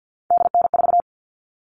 Morse code, heard as a keyed tone, about 48 wpm, 700 Hz. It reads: DN4
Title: All Cut Numbers - 3-Digit 50wpm